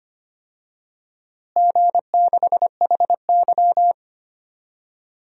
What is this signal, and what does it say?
Morse code, heard as a keyed tone, about 25 wpm, 700 Hz. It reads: G6HY
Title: DXCC-100 Call Signs - 4 Characters - Repeated 2x - 25wpm